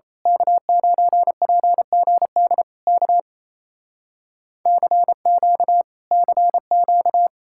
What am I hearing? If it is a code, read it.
Morse code, heard as a keyed tone, about 27 wpm, 700 Hz. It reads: K9PGD K CQ CQ
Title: Speed-Racing Common QSO Phrases Encore - 28wpm